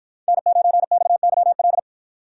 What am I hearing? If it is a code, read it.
Morse code, heard as a keyed tone, about 53 wpm, 700 Hz. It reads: N9XYL